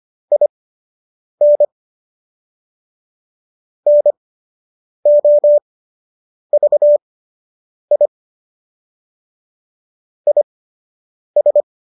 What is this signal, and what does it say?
Morse code, heard as a keyed tone, about 25 wpm, spaced as for 8 wpm, 590 Hz. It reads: IN NOVI IS